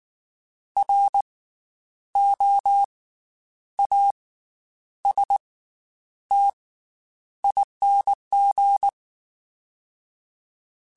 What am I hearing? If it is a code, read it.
Morse code, heard as a keyed tone, about 19 wpm, spaced as for 9 wpm, 800 Hz. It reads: ROASTING
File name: Special INGs / Special INGs-mp3s / Slow ing @18